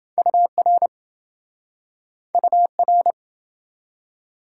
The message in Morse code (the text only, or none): UR UR